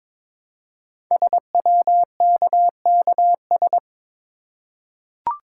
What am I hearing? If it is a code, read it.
Morse code, heard as a keyed tone, about 22 wpm, 700 Hz. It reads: SWKKS
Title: Code Groups - 5 Letters - 22wpm